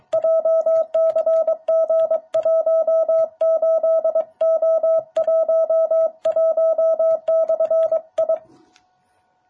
Morse code, heard as a keyed tone, about 23 wpm, 660 Hz. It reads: JCG18O11/I
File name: jh8cbh / 2021cw